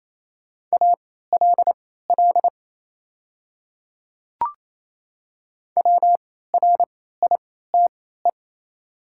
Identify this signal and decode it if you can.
Morse code, heard as a keyed tone, about 27 wpm, spaced as for 16 wpm, 700 Hz. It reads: ALL WRITE
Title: 3x Character Spacing - Top 200 Words - 28wpm